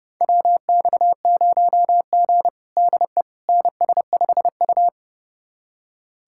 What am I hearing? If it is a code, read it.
Morse code, heard as a keyed tone, about 29 wpm, 700 Hz. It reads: WX0G DE NS5U
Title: USA Callsign DE Callsign - 4 Characters - 30wpm